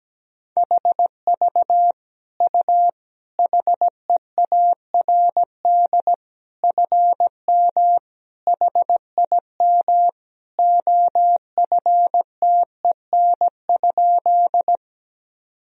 Morse code, heard as a keyed tone, about 17 wpm, 700 Hz. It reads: HV U HEARD FM HIM OFTEN?
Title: Sentences from Top 400 Words and Abbreviations 17wpm